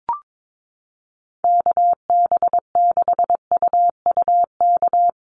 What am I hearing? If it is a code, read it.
Morse code, heard as a keyed tone, about 22 wpm, 700 Hz. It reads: KB6UUK
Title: USA Call Signs Repeated 3x 22wpm